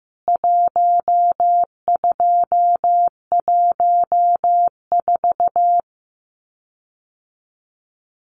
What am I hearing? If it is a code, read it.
Morse code, heard as a keyed tone, about 15 wpm, 700 Hz. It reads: 1214
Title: CWT Exchange 15wpm